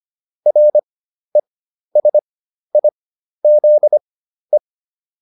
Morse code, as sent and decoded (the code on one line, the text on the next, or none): .-. ... .. --.. .
RESIZE